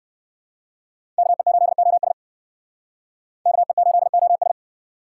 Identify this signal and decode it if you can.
Morse code, heard as a keyed tone, about 60 wpm, 700 Hz. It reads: KE8OS KE8OS